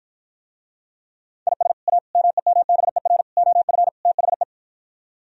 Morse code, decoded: IS A MEMBER OF THE

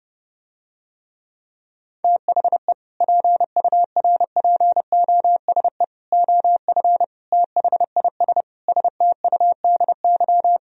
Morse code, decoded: THE PURPOSE OF THIS STUDY

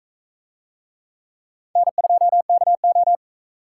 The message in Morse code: N2KO

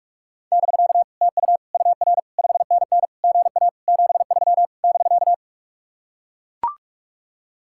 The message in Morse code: <BK> TU UR 5NN GA 73 <BK>

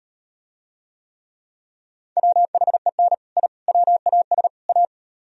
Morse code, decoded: WHEN I WAS A

